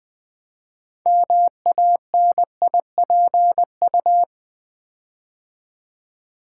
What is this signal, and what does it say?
Morse code, heard as a keyed tone, about 20 wpm, 700 Hz. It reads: MANIPU
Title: Build A Word - 6 Letters At A Time - 12-Letter Words - 20wpm